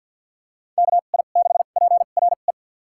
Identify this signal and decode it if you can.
Morse code, heard as a keyed tone, about 52 wpm, spaced as for 36 wpm, 700 Hz. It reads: KI6PRE
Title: USA Call Signs - 2x Character Spacing - 50wpm